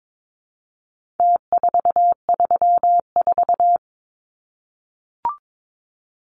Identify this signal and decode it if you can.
Morse code, heard as a keyed tone, about 22 wpm, 700 Hz. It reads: T434
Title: Numbers - Contest Style - 4-Digit 22wpm